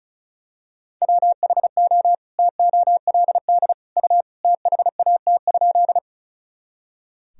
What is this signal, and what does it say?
Morse code, heard as a keyed tone, about 34 wpm, 700 Hz. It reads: WHO TOLD U THAT?
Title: Sentences from Top 400 Words and Abbreviations 35wpm